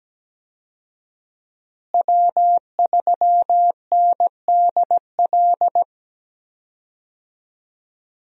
Morse code, heard as a keyed tone, about 17 wpm, 700 Hz. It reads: W3NDL
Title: USA Call Signs - 5 Characters - Repeated 3x - 17wpm